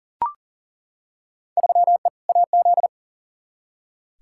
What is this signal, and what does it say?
Morse code, heard as a keyed tone, about 40 wpm, 700 Hz. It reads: 3E AZ
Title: Field Day Exchanges Encore - 40wpm